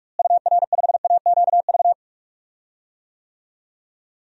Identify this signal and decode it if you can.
Morse code, heard as a keyed tone, about 41 wpm, 700 Hz. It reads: UR5AQV
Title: DXCC-100 Call Signs - 6 Characters - Repeated 2x - 45wpm